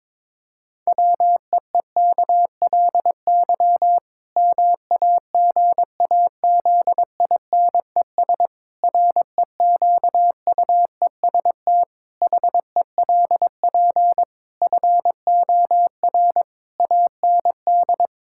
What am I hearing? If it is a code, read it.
Morse code, heard as a keyed tone, about 22 wpm, 700 Hz. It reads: WEEKLY MAGAZINES REQUEST HELP FOR AND
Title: Pangram Challenge v2 - 22wpm